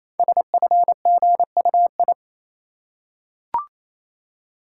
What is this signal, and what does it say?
Morse code, heard as a keyed tone, about 28 wpm, 700 Hz. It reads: SFGUI
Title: Code Groups - 5 Letter:Numbers - 28wpm